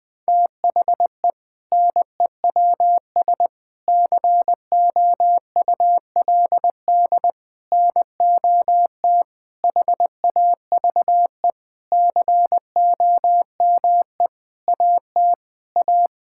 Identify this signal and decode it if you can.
Morse code, heard as a keyed tone, about 21 wpm, 700 Hz. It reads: THE NEWS COULD NOT HAVE COME AT A
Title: Sentences with 8-Word Phrases - Spoken Beginning - 20wpm